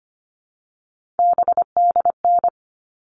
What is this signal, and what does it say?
Morse code, heard as a keyed tone, about 25 wpm, 700 Hz. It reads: BDN
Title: All Cut Numbers - 3-Digit 25wpm